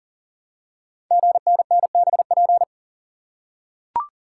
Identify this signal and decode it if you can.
Morse code, heard as a keyed tone, about 43 wpm, 700 Hz. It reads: GNNBP
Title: Code Groups - 5 Letters - 40wpm